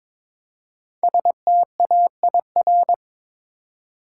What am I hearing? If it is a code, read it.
Morse code, heard as a keyed tone, about 22 wpm, 700 Hz. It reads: STAIR
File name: 5 Letter Words - First 3 Letters Given - 22wpm